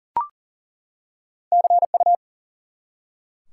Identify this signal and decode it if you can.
Morse code, heard as a keyed tone, about 40 wpm, 700 Hz. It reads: CU
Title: Top 90 QSO Elements 40wpm